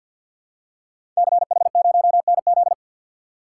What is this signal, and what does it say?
Morse code, heard as a keyed tone, about 52 wpm, 700 Hz. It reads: CH0NZ
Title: Code Groups - 5 Letter:Numbers - 50wpm